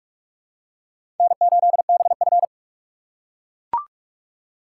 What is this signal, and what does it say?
Morse code, heard as a keyed tone, about 43 wpm, 700 Hz. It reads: N8BF
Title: USA Call Signs - 4 Characters - 6x Character Spacing - 45wpm